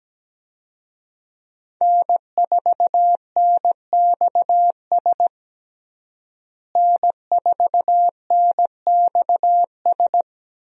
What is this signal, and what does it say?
Morse code, heard as a keyed tone, about 17 wpm, 700 Hz. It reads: N4NXS N4NXS
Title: USA Call Signs - 5 Characters - Repeated 2x - 17wpm